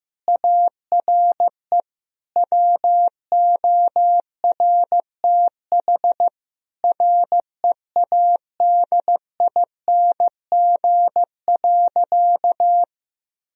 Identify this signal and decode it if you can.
Morse code, heard as a keyed tone, about 15 wpm, 700 Hz. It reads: ARE WORTH READING.
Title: Sentences from Top 1000 Words - Review Part 3 - 15wpm